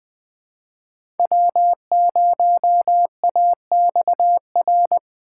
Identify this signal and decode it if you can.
Morse code, heard as a keyed tone, about 20 wpm, 700 Hz. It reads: W0AXR
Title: USA Call Signs - 6x Character Spacing - 20wpm